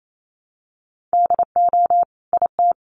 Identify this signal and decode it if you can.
Morse code, heard as a keyed tone, about 28 wpm, 700 Hz. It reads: DO IT
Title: Sentences from Top 1300 Words 28wpm